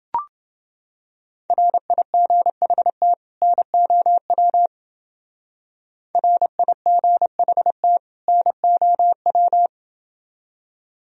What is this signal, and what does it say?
Morse code, heard as a keyed tone, about 28 wpm, 700 Hz. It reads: RIGHT NOW RIGHT NOW